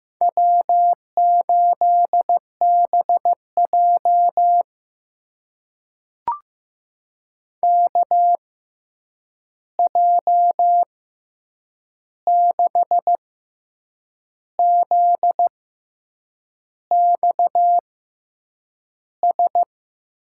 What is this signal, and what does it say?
Morse code, heard as a keyed tone, about 15 wpm, spaced as for 7 wpm, 700 Hz. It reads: W8BJ KJ6ZXS
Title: USA Call Signs - 6x Character Spacing - 15wpm